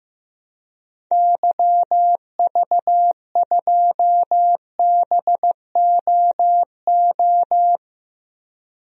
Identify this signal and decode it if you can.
Morse code, heard as a keyed tone, about 15 wpm, 700 Hz. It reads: YV2BOO
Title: DXCC-100 Call Signs - 6 Characters - Repeated 2x - 15wpm